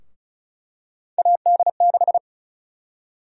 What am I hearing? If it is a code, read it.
Morse code, heard as a keyed tone, about 36 wpm, 700 Hz. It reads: AD6
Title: USA Call Sign Prefixes - 35wpm